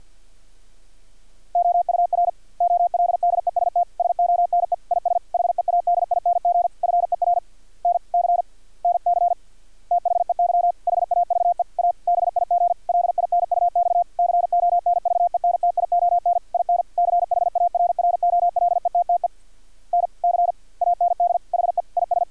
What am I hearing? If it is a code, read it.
Morse code, heard as a keyed tone, about 50 wpm, 700 Hz. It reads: OUR OLDEST SONE IS HEADING FER N C NC THEY HAVE A BIG LINUX CONVENTION IN CHARROLETTE N C AND HE IS
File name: archived / CRSnet